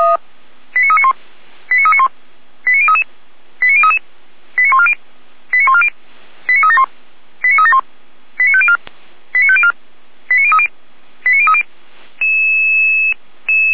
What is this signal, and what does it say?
Five-tone selective call (ZVEI), heard as quick runs of five tones, tones of about 80 ms. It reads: 89391 89391 89033 89033 89140 89140 89371 89371 89504 89504 89030 89030